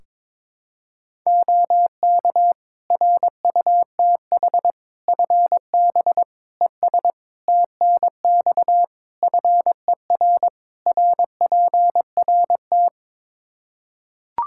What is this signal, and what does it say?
Morse code, heard as a keyed tone, about 22 wpm, 700 Hz. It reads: OK RUTH FB ES TNX FER RPRT
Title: Practice QSO - Exchange 2 - Info - 22wpm